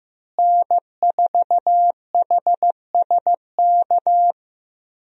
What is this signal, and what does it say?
Morse code, heard as a keyed tone, about 15 wpm, 700 Hz. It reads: N4HSK
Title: USA Call Signs - 5 Characters - Repeated 3x - 15wpm